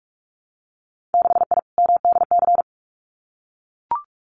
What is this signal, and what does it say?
Morse code, heard as a keyed tone, about 44 wpm, 700 Hz. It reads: 6I MDC
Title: Winter Field Day Exchanges Encore 45wpm